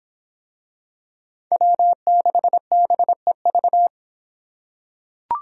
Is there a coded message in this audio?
Yes: Morse code from 1.5 to 3.9 s, heard as a keyed tone, about 26 wpm, 700 Hz. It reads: W6BEV